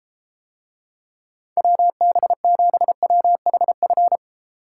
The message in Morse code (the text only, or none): WB7WHF